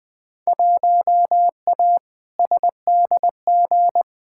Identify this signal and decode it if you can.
Morse code, heard as a keyed tone, about 20 wpm, 700 Hz. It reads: 1A SDG
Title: Field Day Exchanges Encore - 20wpm